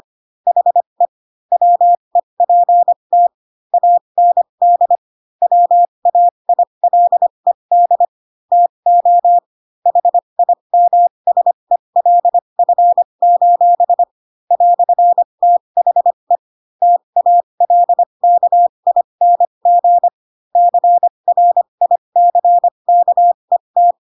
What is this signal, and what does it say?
Morse code, heard as a keyed tone, about 25 wpm, spaced as for 22 wpm, 700 Hz. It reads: HE WEPT AND WAILED TO HIMSELF: "THE TALKING CRICKET